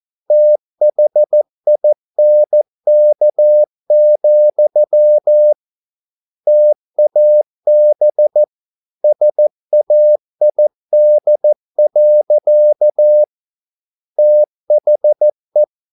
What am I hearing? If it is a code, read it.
Morse code, heard as a keyed tone, about 14 wpm, 600 Hz. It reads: THINK, TAB SAID. THE